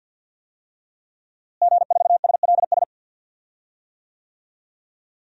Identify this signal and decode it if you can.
Morse code, heard as a keyed tone, about 47 wpm, 700 Hz. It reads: G4SLS
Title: DXCC-100 Call Signs - 5 Characters - Repeated 3x - 50wpm